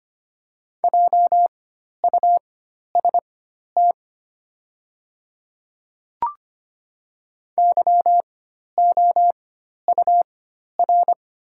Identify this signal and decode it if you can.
Morse code, heard as a keyed tone, about 25 wpm, spaced as for 12 wpm, 700 Hz. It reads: JUST YOUR
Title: 4x Character Spacing - Top 300 Words - 25wpm